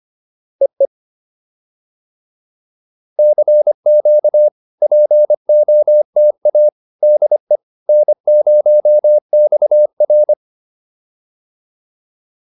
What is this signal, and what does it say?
Morse code, heard as a keyed tone, about 25 wpm, 600 Hz. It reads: EE CQ POTA DE N0XR